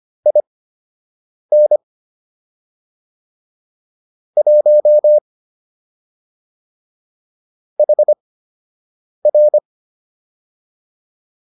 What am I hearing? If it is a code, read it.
Morse code, heard as a keyed tone, about 25 wpm, spaced as for 7 wpm, 600 Hz. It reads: IN 1 HR